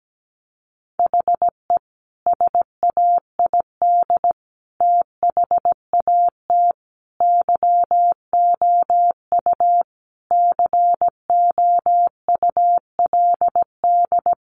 Morse code, heard as a keyed tone, about 17 wpm, 700 Hz. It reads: HE SAID THAT YOU COULD
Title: Sentences from Top 1000 Words 17wpm